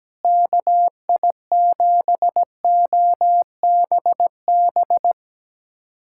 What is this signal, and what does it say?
Morse code, heard as a keyed tone, about 17 wpm, 700 Hz. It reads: KI7OBB
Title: USA Callsign DE Callsign - 6 Characters - 17wpm